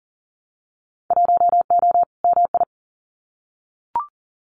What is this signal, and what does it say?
Morse code, heard as a keyed tone, about 40 wpm, 700 Hz. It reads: 1O MI
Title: Winter Field Day Exchanges Encore 40wpm